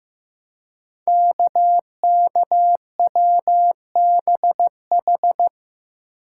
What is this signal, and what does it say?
Morse code, heard as a keyed tone, about 15 wpm, 700 Hz. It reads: KKWBH
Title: Code Groups - 5 Letters - 15wpm